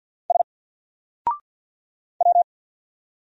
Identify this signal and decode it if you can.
Morse code, heard as a keyed tone, about 48 wpm, 700 Hz. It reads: S W
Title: Single Character (Letter:Number:Punctuation:Prosign) Alt 50wpm